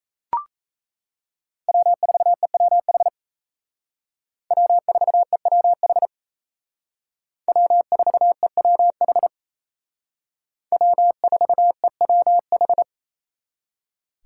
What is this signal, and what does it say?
Morse code, heard as a keyed tone, about 34 wpm, 700 Hz. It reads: W4EWH W4EWH W4EWH W4EWH
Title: Speed Racing - USA 1x3 Call Signs 28wpm